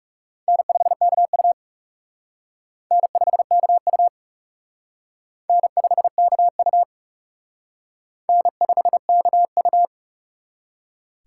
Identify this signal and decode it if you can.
Morse code, heard as a keyed tone, about 37 wpm, 700 Hz. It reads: N5KU N5KU N5KU N5KU
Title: Speed Racing - USA Call Signs 30wpm